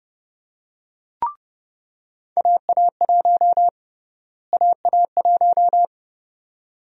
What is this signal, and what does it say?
Morse code, heard as a keyed tone, about 30 wpm, 700 Hz. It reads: AA1 AA1
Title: USA Call Sign Prefixes - Repeated 2x - 30wpm